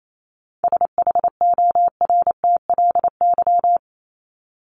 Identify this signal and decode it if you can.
Morse code, heard as a keyed tone, about 28 wpm, 700 Hz. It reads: SHORTLY
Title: LY Suffix - Top 100 Words with Base Words 28wpm